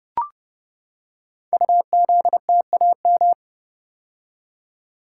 Morse code, heard as a keyed tone, about 30 wpm, 700 Hz. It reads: UZTAM
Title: Code Groups - 5 Letters - 30wpm